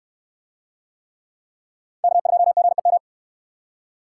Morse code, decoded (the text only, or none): K2GEM